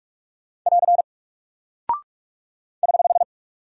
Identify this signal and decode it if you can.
Morse code, heard as a keyed tone, about 44 wpm, 700 Hz. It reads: <AR> <HH>
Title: Single Character (Letter:Number:Punctuation:Prosign) Alt 45wpm